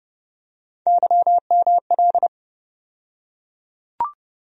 Morse code, heard as a keyed tone, about 30 wpm, 700 Hz. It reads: YML
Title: USA Call Sign Postfixes - 3 Letters 30wpm